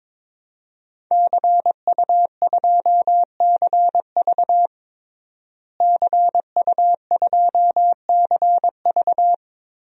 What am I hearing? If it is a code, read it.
Morse code, heard as a keyed tone, about 22 wpm, 700 Hz. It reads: CU2CV CU2CV
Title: DXCC-100 Call Signs - 5 Characters - Repeated 3x - 22wpm